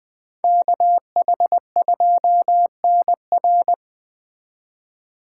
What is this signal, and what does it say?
Morse code, heard as a keyed tone, about 20 wpm, 700 Hz. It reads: KH2NR